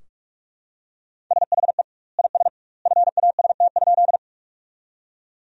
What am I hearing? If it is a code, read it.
Morse code, heard as a keyed tone, about 43 wpm, 700 Hz. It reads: SHE IS FAST?
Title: 2x Word-Spacing - Sentences from Top 400 Words - 45wpm